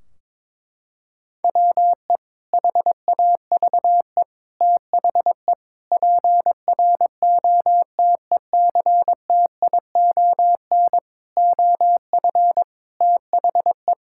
Morse code, heard as a keyed tone, about 22 wpm, 700 Hz. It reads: WE HAVE THE PROTECTION OF THE
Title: Sentences from Top 1600 Words 22wpm